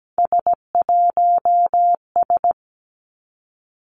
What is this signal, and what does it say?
Morse code, heard as a keyed tone, about 17 wpm, 700 Hz. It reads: S1S
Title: Three Challenging Letter-Number Combinations 17wpm